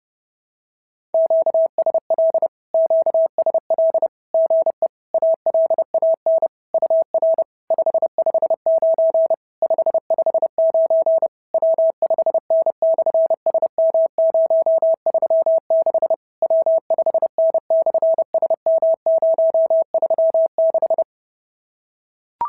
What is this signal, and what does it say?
Morse code, heard as a keyed tone, about 30 wpm, 650 Hz. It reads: QSL QSL GE ALAN UR 559 559 W5N/SM036 W5N/SM036
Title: SOTA QSO Practice 30wpm